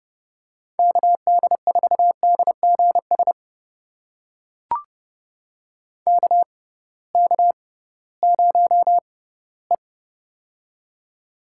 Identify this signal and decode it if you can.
Morse code, heard as a keyed tone, about 31 wpm, spaced as for 18 wpm, 700 Hz. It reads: KD4DGS KK0E
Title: USA Call Signs - 6x Character Spacing - 30wpm